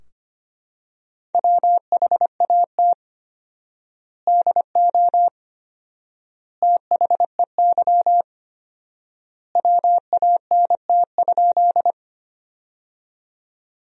Morse code, heard as a keyed tone, about 25 wpm, 700 Hz. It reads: WHAT DO THEY WANT?
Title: 4x Word-Spacing - Sentences from Top 200 Words - 25wpm